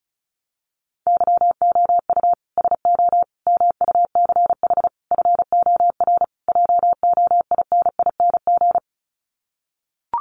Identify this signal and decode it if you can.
Morse code, heard as a keyed tone, about 34 wpm, 700 Hz. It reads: YOU SO MUCH FOR JOINING
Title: Sets of 5 Words Encore2 35wpm